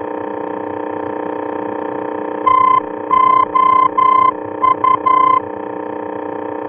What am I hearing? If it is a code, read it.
Morse code, heard as a keyed tone, about 11 wpm, 1030 Hz, with noise. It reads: TOU
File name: france / mx / TOU-F-VOR